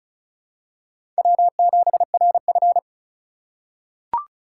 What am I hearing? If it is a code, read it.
Morse code, heard as a keyed tone, about 36 wpm, 700 Hz. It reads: W7RF
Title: USA Call Signs - 2x Character Spacing - 35wpm